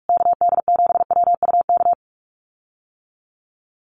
Morse code, heard as a keyed tone, about 44 wpm, 700 Hz. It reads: KD7WUK
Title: USA Call Signs Repeated 3x 45wpm